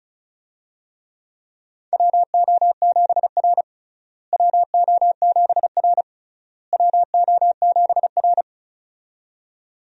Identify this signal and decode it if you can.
Morse code, heard as a keyed tone, about 34 wpm, 700 Hz. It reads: WO7R WO7R WO7R
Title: USA Call Signs - 4 Characters - Repeated 3x - 35wpm